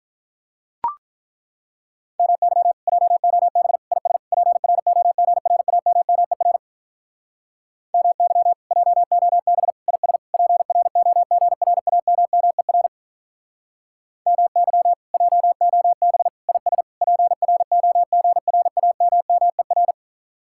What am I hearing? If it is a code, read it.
Morse code, heard as a keyed tone, about 43 wpm, 700 Hz. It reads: MY JOB IS PROGRAMMER MY JOB IS PROGRAMMER MY JOB IS PROGRAMMER
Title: Speed-Racing Common QSO Phrases - 35wpm